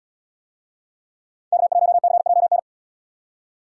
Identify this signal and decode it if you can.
Morse code, heard as a keyed tone, about 71 wpm, 700 Hz. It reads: K1CJN